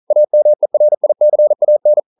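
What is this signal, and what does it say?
Morse code, heard as a keyed tone, about 42 wpm, 600 Hz. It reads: AMERICAN
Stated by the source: Morse code practice at 41 wpm, from CW med Gnister.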